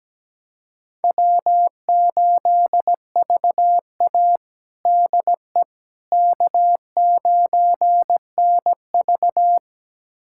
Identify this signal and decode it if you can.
Morse code, heard as a keyed tone, about 17 wpm, 700 Hz. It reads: W8VA DE K9NV
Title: USA Callsign DE Callsign - 4 Characters - 17wpm